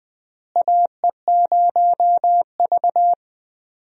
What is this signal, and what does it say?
Morse code, heard as a keyed tone, about 20 wpm, 700 Hz. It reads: AE0V
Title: USA Call Signs - 4 Characters - Repeated 3x - 20wpm